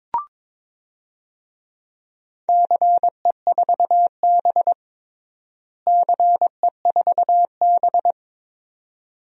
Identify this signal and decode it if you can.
Morse code, heard as a keyed tone, about 23 wpm, 700 Hz. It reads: CE4B CE4B
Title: DXCC-100 Call Signs - 4 Characters - Repeated 2x - 22wpm